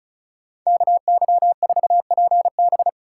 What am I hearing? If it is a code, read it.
Morse code, heard as a keyed tone, about 44 wpm, 700 Hz. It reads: KY4PB
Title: Speed Racing - USA 2x2 Call Signs 35wpm